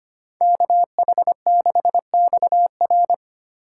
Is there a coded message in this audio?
Yes: Morse code from 0.4 to 3.1 s, heard as a keyed tone, about 33 wpm, 700 Hz. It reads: KH6XR